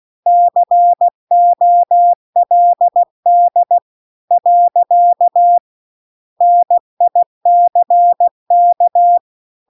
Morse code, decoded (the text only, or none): COLD . NICK